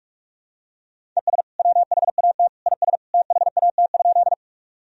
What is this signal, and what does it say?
Morse code, heard as a keyed tone, about 46 wpm, 700 Hz. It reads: ES WHAT IS THAT?
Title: Sentences from Top 100 Words and Abbreviations 45wpm